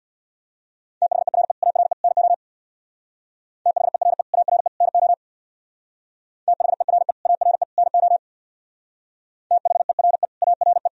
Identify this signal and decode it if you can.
Morse code, heard as a keyed tone, about 60 wpm, 700 Hz. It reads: THERE ARE NO THERE ARE NO THERE ARE NO THERE ARE
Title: Speed-Racing - Sets of 3 Words 50wpm